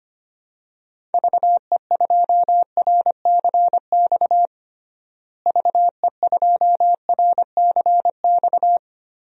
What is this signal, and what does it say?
Morse code, heard as a keyed tone, about 25 wpm, 700 Hz. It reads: VE2RCX VE2RCX
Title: DXCC-100 Call Signs - 6 Characters - Repeated 3x - 25wpm